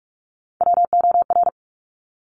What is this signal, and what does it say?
Morse code, heard as a keyed tone, about 46 wpm, 700 Hz. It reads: WOR
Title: Top Three Letter Combinations in Words 45wpm